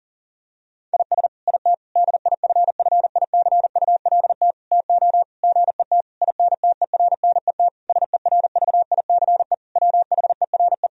Morse code, heard as a keyed tone, about 40 wpm, 700 Hz. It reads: IS IT DIFFICULT TO GET INTERNET SERVICE WHERE
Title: Sentences from Top 1000 Words - Review Part 4 - 40wpm